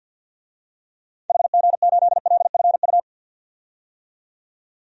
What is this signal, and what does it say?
Morse code, heard as a keyed tone, about 51 wpm, 700 Hz. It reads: HG8LFU
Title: DXCC-100 Call Signs - 6 Characters - 50wpm